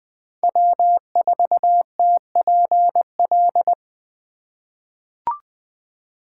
Morse code, heard as a keyed tone, about 20 wpm, 700 Hz. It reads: W4TPL
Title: USA Call Signs - 5 Characters - 2x Character Spacing - 20wpm